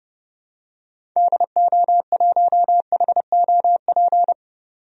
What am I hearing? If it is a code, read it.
Morse code, heard as a keyed tone, about 30 wpm, 700 Hz. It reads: DO1HOP
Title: DXCC-100 Call Signs - 6 Characters - 30wpm